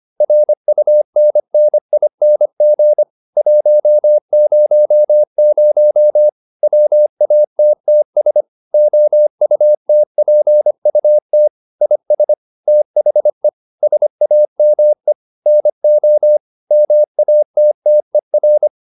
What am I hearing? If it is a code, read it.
Morse code, heard as a keyed tone, about 25 wpm, 600 Hz. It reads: RUNNING 100 WATTS OUTPUT IS THE SAME NO MATTER